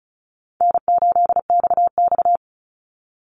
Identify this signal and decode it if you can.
Morse code, heard as a keyed tone, about 36 wpm, 700 Hz. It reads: N8XX
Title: Call Signs - 35wpm